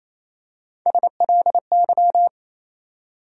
Morse code, heard as a keyed tone, about 28 wpm, 700 Hz. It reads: SLY
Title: USA Call Sign Postfixes - 3 Letters 28wpm